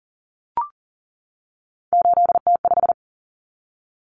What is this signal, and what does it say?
Morse code, heard as a keyed tone, about 40 wpm, 700 Hz. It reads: 8T5
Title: Common Cut Numbers - 3-Digit 40wpm